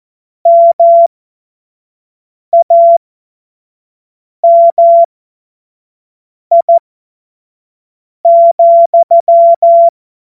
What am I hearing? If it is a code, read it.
Morse code, heard as a keyed tone, about 14 wpm, spaced as for 5 wpm, 680 Hz. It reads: MAMI,
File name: Morsekurs / Witze / Witz 25. Mai 2021, 25 F70